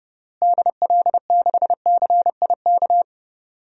Morse code, heard as a keyed tone, about 30 wpm, 700 Hz. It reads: DL6CIK
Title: DXCC-100 Call Signs - 6 Characters - Repeated 3x - 30wpm